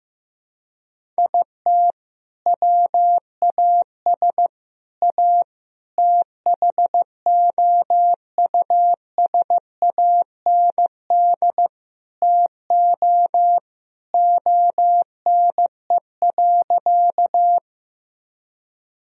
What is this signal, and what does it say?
Morse code, heard as a keyed tone, about 16 wpm, 700 Hz. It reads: IT WAS A THOUSAND TO ONE.
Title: Sentences from Top 1000 Words - Review Part 3 - 15wpm